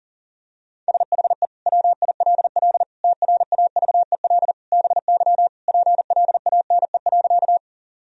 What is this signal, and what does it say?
Morse code, heard as a keyed tone, about 40 wpm, 700 Hz. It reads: SHE WILL TRAVEL BY PLANE.
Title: Sentences from Top 1000 Words - Review Part 3 - 40wpm